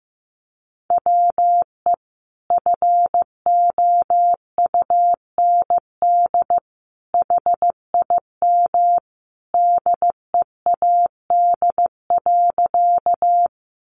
Morse code, heard as a keyed tone, about 15 wpm, 700 Hz. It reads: WE FOUND HIM DEAD.
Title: Sentences from Top 1500 Words 15wpm